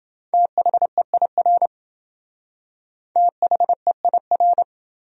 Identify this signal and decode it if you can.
Morse code, heard as a keyed tone, about 27 wpm, 700 Hz. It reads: THEIR THEIR